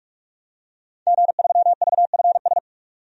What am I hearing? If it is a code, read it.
Morse code, heard as a keyed tone, about 43 wpm, 700 Hz. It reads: G3VFS
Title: DXCC-100 Call Signs - 5 Characters - 45wpm